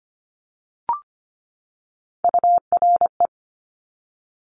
Morse code, heard as a keyed tone, about 25 wpm, 700 Hz. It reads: URE
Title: Top Three Letter Combinations in Words 25wpm